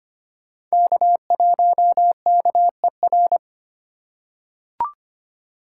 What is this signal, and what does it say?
Morse code, heard as a keyed tone, about 25 wpm, 700 Hz. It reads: K1KER
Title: USA Call Signs - 5 Characters - 4x Character Spacing - 25wpm